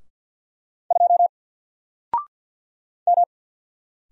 Morse code, heard as a keyed tone, about 48 wpm, 700 Hz. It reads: 2 M
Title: Single Letter-Number Alt 50wpm